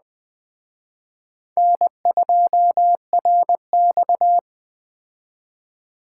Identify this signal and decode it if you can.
Morse code, heard as a keyed tone, about 20 wpm, 700 Hz. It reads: N2RX
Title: Speed Racing - USA 1x2 Call Signs 17wpm